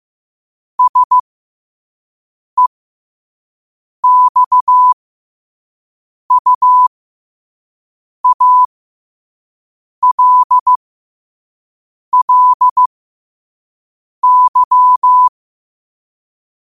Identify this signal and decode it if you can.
Morse code, heard as a keyed tone, about 15 wpm, spaced as for 5 wpm, 1000 Hz. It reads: SEXUALLY
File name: qotdfiles / 5wpm